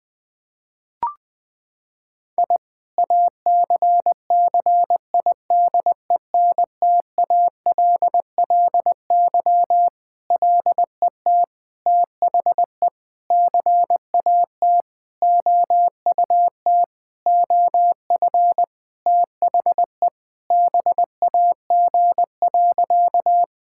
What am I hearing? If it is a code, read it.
Morse code, heard as a keyed tone, about 21 wpm, 700 Hz. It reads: I ACCIDENTALLY LET THE CAT OUT OF THE BAG.
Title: Sentences with 7-Word Phrases - 20wpm